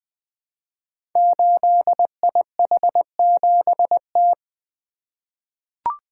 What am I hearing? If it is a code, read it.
Morse code, heard as a keyed tone, about 20 wpm, 700 Hz. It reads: 8IH7T
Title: Code Groups - 5 Letter:Numbers - 20wpm